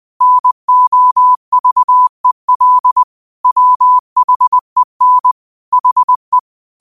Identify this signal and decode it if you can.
Morse code, heard as a keyed tone, about 20 wpm, 1000 Hz. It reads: NOVEL WHEN HE
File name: qotdfiles / 20wpm